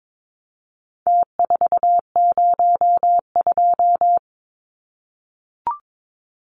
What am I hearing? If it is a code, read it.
Morse code, heard as a keyed tone, about 22 wpm, 700 Hz. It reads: T402
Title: Numbers - Contest Style - 4-Digit 22wpm